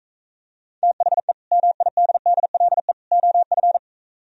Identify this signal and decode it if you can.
Morse code, heard as a keyed tone, about 44 wpm, 700 Hz. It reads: THE MIDDLE OF